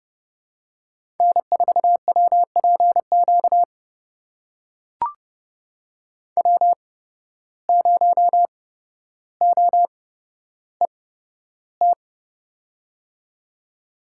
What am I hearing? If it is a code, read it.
Morse code, heard as a keyed tone, about 29 wpm, spaced as for 13 wpm, 700 Hz. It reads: N4WPQ W0OET
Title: USA Call Signs - 5 Characters - 8x Character Spacing - 30wpm